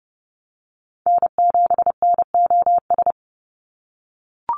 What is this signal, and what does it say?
Morse code, heard as a keyed tone, about 30 wpm, 700 Hz. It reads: N7NOS
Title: USA Call Signs 30wpm